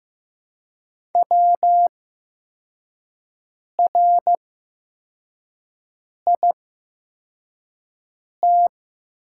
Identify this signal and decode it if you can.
Morse code, heard as a keyed tone, about 15 wpm, spaced as for 4 wpm, 700 Hz. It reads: WRIT
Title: 8x Character Spacing - Top 200 Words - 15wpm